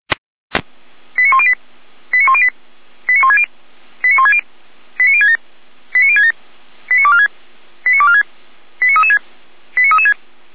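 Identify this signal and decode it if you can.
Five-tone selective call (ZVEI), heard as quick runs of five tones, tones of about 70 ms. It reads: 89108 89108 89160 89160 89067 89067 89246 89246 89336 89336
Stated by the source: ZVEI five-tone call 89170, received on SDR.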